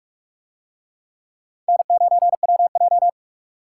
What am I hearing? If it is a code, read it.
Morse code, heard as a keyed tone, about 43 wpm, 700 Hz. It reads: N9WJ